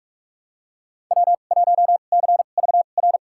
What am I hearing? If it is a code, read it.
Morse code, heard as a keyed tone, about 46 wpm, spaced as for 33 wpm, 700 Hz. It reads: W1CVR
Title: USA Call Signs - 5 Characters - 2x Character Spacing - 45wpm